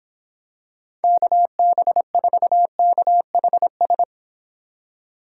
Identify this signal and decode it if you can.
Morse code, heard as a keyed tone, about 26 wpm, 700 Hz. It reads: KB4KHS
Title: Speed Racing - USA 2x3 Call Signs 22wpm